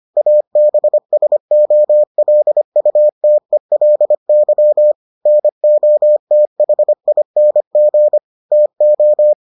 Morse code, heard as a keyed tone, about 25 wpm, 600 Hz. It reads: ABSOLUTELY NOTHING TO